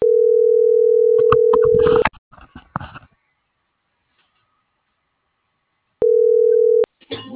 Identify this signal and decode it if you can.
Ringback tone (North America), 440 + 480 Hz.
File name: CENTRAL SOTOVA / out alejandro.rodriguez - .1031